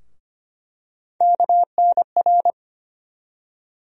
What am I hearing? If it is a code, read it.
Morse code, heard as a keyed tone, about 25 wpm, 700 Hz. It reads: KNR
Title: USA Call Sign Postfixes - 3 Letters 25wpm